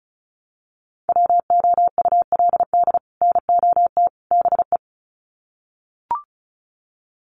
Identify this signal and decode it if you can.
Morse code, heard as a keyed tone, about 34 wpm, 700 Hz. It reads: WOULD NOT BE
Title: Sets of 3 Words 35wpm